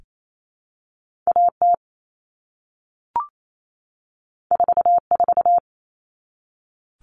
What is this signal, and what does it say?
Morse code, heard as a keyed tone, about 28 wpm, 700 Hz. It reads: AT 44
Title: All Cut Numbers - 2-Digit 28wpm